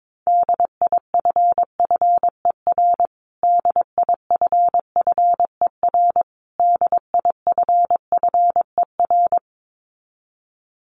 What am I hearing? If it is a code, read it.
Morse code, heard as a keyed tone, about 22 wpm, 700 Hz. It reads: DIFFER DIFFER DIFFER
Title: Top 500 Words Repeated 22wpm